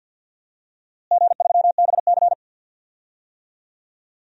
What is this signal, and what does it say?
Morse code, heard as a keyed tone, about 50 wpm, 700 Hz. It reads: G3BC
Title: DXCC-100 Call Signs - 4 Characters - Repeated 3x - 50wpm